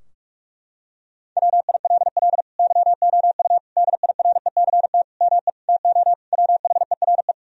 Morse code, heard as a keyed tone, about 46 wpm, 700 Hz. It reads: WILL YOU DIRECT ME TO WHERE
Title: Sentences from Top 1000 Words - Review Part 2 - 45wpm